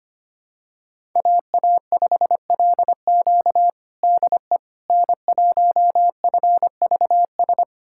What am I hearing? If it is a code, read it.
Morse code, heard as a keyed tone, about 25 wpm, 700 Hz. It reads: AA5LQ DE N1FVS
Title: USA Callsign DE Callsign - 5 Characters - 25wpm